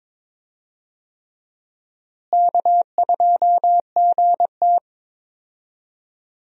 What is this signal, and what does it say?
Morse code, heard as a keyed tone, about 22 wpm, 700 Hz. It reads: K2GT
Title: USA Call Signs - 4 Characters - Repeated 3x - 22wpm